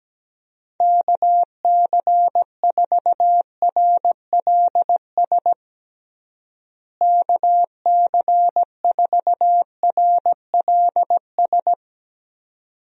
Morse code, heard as a keyed tone, about 17 wpm, 700 Hz. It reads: KC4RLS KC4RLS
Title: USA Call Signs - 6 Characters - Repeated 2x - 17wpm